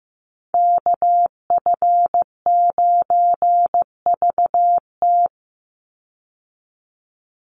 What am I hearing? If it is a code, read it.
Morse code, heard as a keyed tone, about 15 wpm, 700 Hz. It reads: KF9VT
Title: USA Call Signs 15wpm